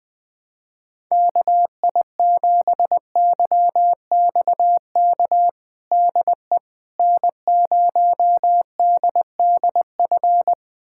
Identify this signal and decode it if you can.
Morse code, heard as a keyed tone, about 21 wpm, 700 Hz. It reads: KI7YXK DE N0DDF
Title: USA Callsign DE Callsign 20wpm